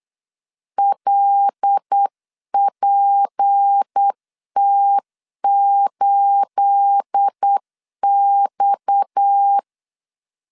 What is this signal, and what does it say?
Morse code, heard as a keyed tone, about 8 wpm, 800 Hz. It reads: LPT8X
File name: LEZIONI / 9 286 8wpm 3425